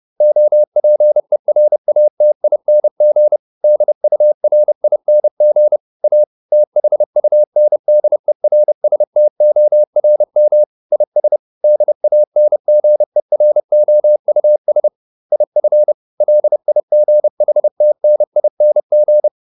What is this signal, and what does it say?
Morse code, heard as a keyed tone, about 30 wpm, 600 Hz. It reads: OPERATING DURING A THUNDERSTORM IS DANGEROUS IF LIGHTNING